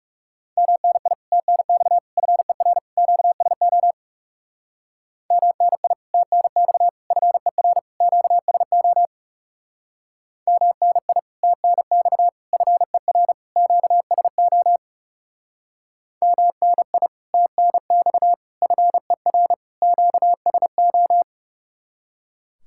Morse code, heard as a keyed tone, about 39 wpm, 700 Hz. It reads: MNI TNX FER QSO MNI TNX FER QSO MNI TNX FER QSO MNI TNX FER QSO